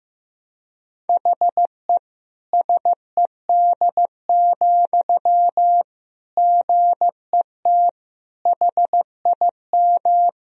Morse code, heard as a keyed tone, about 16 wpm, 700 Hz. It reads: HE SED, GET HIM
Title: Sentences from Top 400 Words and Abbreviations 15wpm